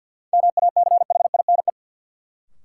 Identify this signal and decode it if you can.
Morse code, heard as a keyed tone, about 53 wpm, 700 Hz. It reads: MACHINE
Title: Sentences from Top 1000 Words - Review Part 2 - New Word Drill - 50wpm